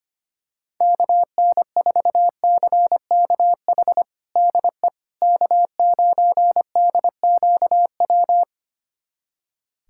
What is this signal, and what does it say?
Morse code, heard as a keyed tone, about 25 wpm, 700 Hz. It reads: KN4CKH DE K9DQW
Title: USA Callsign DE Callsign 25wpm